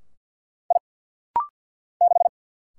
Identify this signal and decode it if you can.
Morse code, heard as a keyed tone, about 48 wpm, 700 Hz. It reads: I 6
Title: Single Letter-Number - Rapid-Fire 50wpm